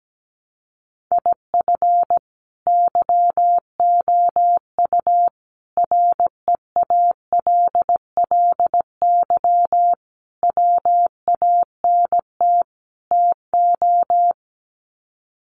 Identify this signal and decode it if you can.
Morse code, heard as a keyed tone, about 17 wpm, 700 Hz. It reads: IF YOU REALLY WANT TO
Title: Sets of 5 Words Encore2 17wpm